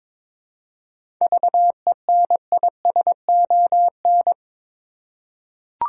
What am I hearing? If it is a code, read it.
Morse code, heard as a keyed tone, about 22 wpm, 700 Hz. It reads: VENISON
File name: Difficult Words with Hints - 22wpm